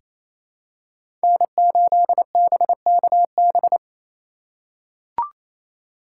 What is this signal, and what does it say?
Morse code, heard as a keyed tone, about 27 wpm, 700 Hz. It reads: N8BKB
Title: USA Call Signs - 3x Character Spacing - 28wpm